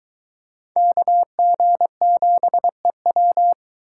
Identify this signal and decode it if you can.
Morse code, heard as a keyed tone, about 22 wpm, 700 Hz. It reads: KG7EW